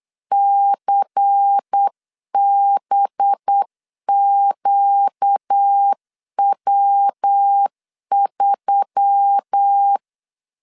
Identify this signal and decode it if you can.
Morse code, heard as a keyed tone, about 9 wpm, 800 Hz. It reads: CBQW3
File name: CW / DET / 1 6 1500C S C 8wpm 5219